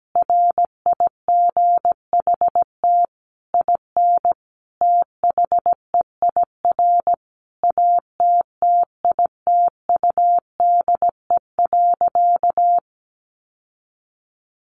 Morse code, heard as a keyed tone, about 17 wpm, 700 Hz. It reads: RIGHT IN THEIR ATTITUDE.
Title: Sentences from Top 1300 Words 17wpm